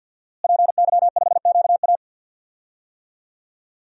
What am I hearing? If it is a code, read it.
Morse code, heard as a keyed tone, about 47 wpm, 700 Hz. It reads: PY5QA
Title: DXCC-100 Call Signs - 5 Characters - Repeated 3x - 50wpm